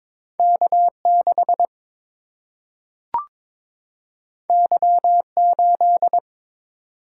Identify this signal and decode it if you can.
Morse code, heard as a keyed tone, about 22 wpm, 700 Hz. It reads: K 6 Y 8
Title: Single-Letter Single-Number 22wpm